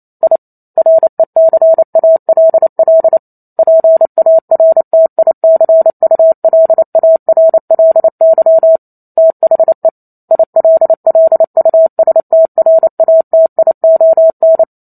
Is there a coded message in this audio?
Yes: Morse code from 0.2 to 14.6 s, heard as a keyed tone, about 28 wpm, 650 Hz. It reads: I RECALL PARTICULARLY THE ILLUSTRATION